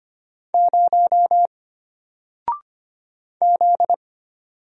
Morse code, heard as a keyed tone, about 25 wpm, 700 Hz. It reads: 0 Z
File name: Single-Letter-Number-25wpm